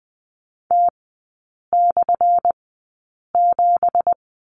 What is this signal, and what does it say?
Morse code, heard as a keyed tone, about 20 wpm, spaced as for 8 wpm, 700 Hz. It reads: T/7
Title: 218 Lesson 35e - Number 0 - Three Characters 20wpm